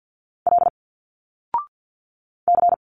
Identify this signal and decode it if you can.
Morse code, heard as a keyed tone, about 47 wpm, 700 Hz. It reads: L C
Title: Single Letters 50wpm